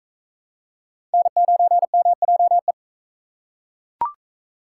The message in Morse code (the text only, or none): N9MJE